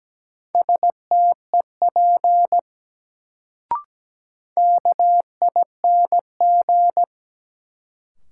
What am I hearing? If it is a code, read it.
Morse code, heard as a keyed tone, about 17 wpm, 700 Hz. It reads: STEP KING
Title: Sentences from Top 400 Words - New Word Drill - 17wpm